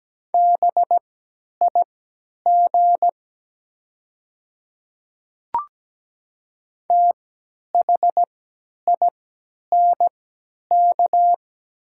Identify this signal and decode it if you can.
Morse code, heard as a keyed tone, about 17 wpm, spaced as for 10 wpm, 700 Hz. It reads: BIG THINK
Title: 3x Character Spacing - Top 200 Words - 17wpm